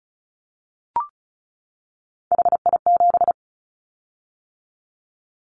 Morse code, heard as a keyed tone, about 36 wpm, 700 Hz. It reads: HI7
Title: Callsign Prefixes for Top 100 DXCC - 35wpm